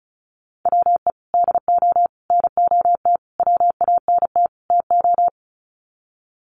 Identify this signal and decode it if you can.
Morse code, heard as a keyed tone, about 34 wpm, 700 Hz. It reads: WE DO NOT WANT TO
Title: Sets of 5 Words 35wpm